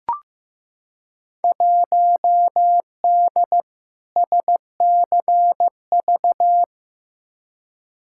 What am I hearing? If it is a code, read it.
Morse code, heard as a keyed tone, about 15 wpm, 700 Hz. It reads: 1D SCV
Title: Field Day Exchanges Encore - 15wpm